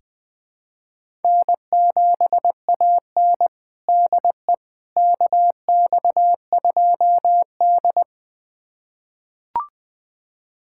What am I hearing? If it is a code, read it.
Morse code, heard as a keyed tone, about 20 wpm, 700 Hz. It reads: N7AN DE KX2D
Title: USA Callsign DE Callsign - 4 Characters - 20wpm